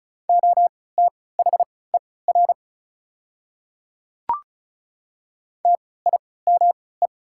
Morse code, heard as a keyed tone, about 36 wpm, spaced as for 20 wpm, 700 Hz. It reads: OTHER TIME
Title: 3x Character Spacing - Top 200 Words - 35wpm